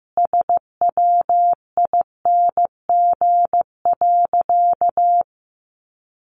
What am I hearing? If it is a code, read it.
Morse code, heard as a keyed tone, about 15 wpm, 700 Hz. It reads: SWING.
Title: Additional Easy Phrases 15wpm